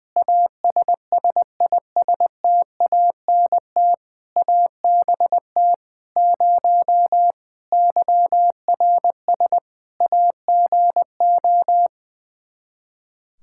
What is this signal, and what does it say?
Morse code, heard as a keyed tone, about 20 wpm, 700 Hz. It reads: ASSISTANT ABT 0 YRS AGO
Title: K1USN SST QSO Practice 20wpm